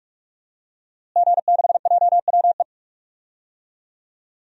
Morse code, heard as a keyed tone, about 48 wpm, 700 Hz. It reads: G6JWE
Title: DXCC-100 Call Signs - 5 Characters - Repeated 3x - 45wpm